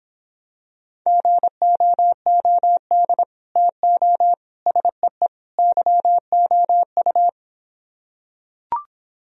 Morse code, heard as a keyed tone, about 26 wpm, 700 Hz. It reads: GOOD TO SEE YOU